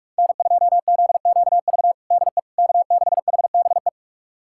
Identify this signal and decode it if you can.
Morse code, heard as a keyed tone, about 42 wpm, 700 Hz. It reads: N2ZQV DE K6HBE
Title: USA Callsign DE Callsign 45wpm